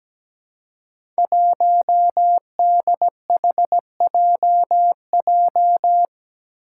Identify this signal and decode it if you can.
Morse code, heard as a keyed tone, about 17 wpm, 700 Hz. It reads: 1DHJJ
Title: Five Challenging Letter-Number Combinations 17wpm